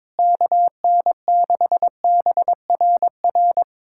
Morse code, heard as a keyed tone, about 22 wpm, 700 Hz. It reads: KN6BRR
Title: USA Callsign DE Callsign 22wpm